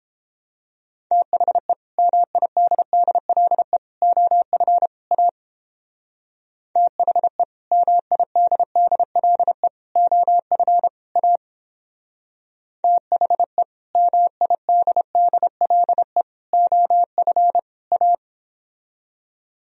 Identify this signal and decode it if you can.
Morse code, heard as a keyed tone, about 29 wpm, 700 Hz. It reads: THE MIDDLE OF A THE MIDDLE OF A THE MIDDLE OF A